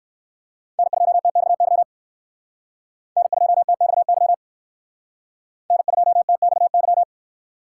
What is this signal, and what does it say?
Morse code, heard as a keyed tone, about 57 wpm, 700 Hz. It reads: N2TXY N2TXY N2TXY